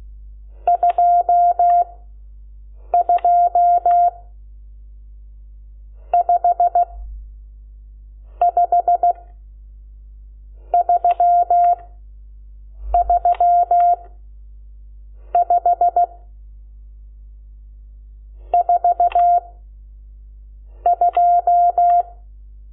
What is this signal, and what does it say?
Morse code, heard as a keyed tone, about 16 wpm, 700 Hz. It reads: 2 2 5 5 3 3 5 4 2
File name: Lecciones / CLASE Nº12